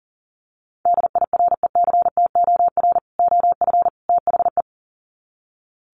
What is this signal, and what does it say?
Morse code, heard as a keyed tone, about 40 wpm, 700 Hz. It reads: DIRECTOR OF THE
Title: Sets of 3 Words 40wpm